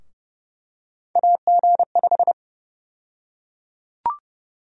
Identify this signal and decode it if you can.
Morse code, heard as a keyed tone, about 30 wpm, 700 Hz. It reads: AG5
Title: USA Call Sign Prefixes - 30wpm